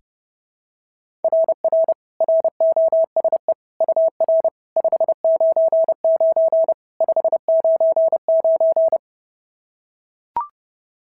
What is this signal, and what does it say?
Morse code, heard as a keyed tone, about 27 wpm, 650 Hz. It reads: RR ROSE UR 599 599
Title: SOTA QSO Practice 30wpm